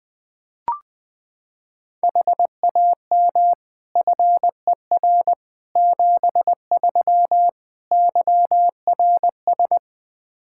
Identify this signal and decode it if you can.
Morse code, heard as a keyed tone, about 20 wpm, 700 Hz. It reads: HAM FER 73 YRS
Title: Practice QSO - Exchange 3 - More Info - 20wpm